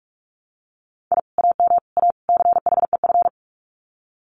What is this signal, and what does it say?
Morse code, heard as a keyed tone, about 44 wpm, 700 Hz. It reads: I AM A CHEF
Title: Common QSO Phrases Encore 45wpm